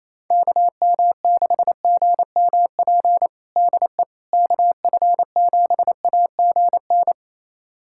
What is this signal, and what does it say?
Morse code, heard as a keyed tone, about 28 wpm, 700 Hz. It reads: KM6GMP DE KF7AGN
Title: USA Callsign DE Callsign - 6 Characters - 28wpm